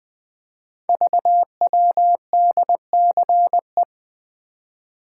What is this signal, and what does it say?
Morse code, heard as a keyed tone, about 20 wpm, 700 Hz. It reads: VWDCE
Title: Code Groups - 5 Letters - 20wpm